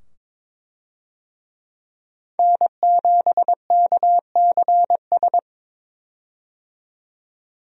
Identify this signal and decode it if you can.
Morse code, heard as a keyed tone, about 21 wpm, 700 Hz. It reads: N7KCS
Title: USA Call Signs - 5 Characters - 22wpm